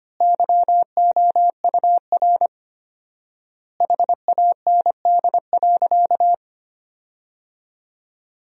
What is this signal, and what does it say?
Morse code, heard as a keyed tone, about 25 wpm, 700 Hz. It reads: YOUR HAND.
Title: 4x Word-Spacing - Sentences from Top 200 Words - 25wpm